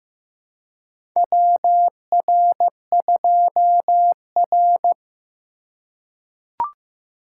Morse code, heard as a keyed tone, about 15 wpm, 700 Hz. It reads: WR2R
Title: USA Call Signs - 4 Characters - 4x Character Spacing - 15wpm